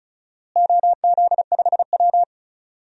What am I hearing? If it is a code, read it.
Morse code, heard as a keyed tone, about 36 wpm, 700 Hz. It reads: OZ5W
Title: DXCC-100 Call Signs - 4 Characters - Repeated 2x - 35wpm